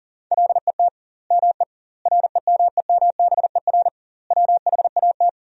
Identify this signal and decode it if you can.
Morse code, heard as a keyed tone, about 40 wpm, 700 Hz. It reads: LET ME REMEMBER WHAT
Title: 2x Word-Spacing - Sentences from Top 400 Words - 40wpm